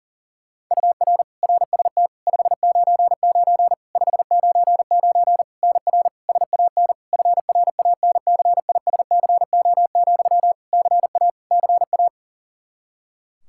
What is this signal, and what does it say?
Morse code, heard as a keyed tone, about 37 wpm, 700 Hz. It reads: UR RST 599 599 NR SAN FRANCISCO, CA CA
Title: Practice QSO - Exchange 1 - Beginning - 40wpm